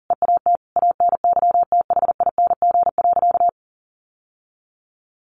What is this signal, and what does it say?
Morse code, heard as a keyed tone, about 40 wpm, 700 Hz. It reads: EAT ANYTHING.
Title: Sentences from Top 1000 Words 40wpm